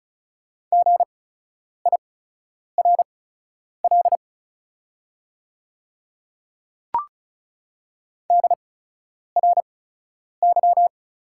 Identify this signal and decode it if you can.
Morse code, heard as a keyed tone, about 36 wpm, spaced as for 10 wpm, 690 Hz. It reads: GIRL DRY
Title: 8x Character Spacing - Top 500 Words - 35wpm